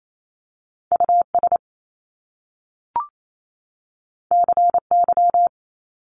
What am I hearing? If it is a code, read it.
Morse code, heard as a keyed tone, about 28 wpm, 700 Hz. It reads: US CY
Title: Top Two Letter Combinations in Words 28wpm